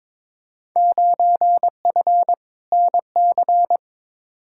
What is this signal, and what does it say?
Morse code, heard as a keyed tone, about 23 wpm, 700 Hz. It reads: 9F NC
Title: Field Day Exchanges Encore - 22wpm